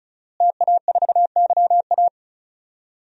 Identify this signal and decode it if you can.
Morse code, heard as a keyed tone, about 36 wpm, 700 Hz. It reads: TA4YA
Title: DXCC-100 Call Signs - 5 Characters - Repeated 3x - 35wpm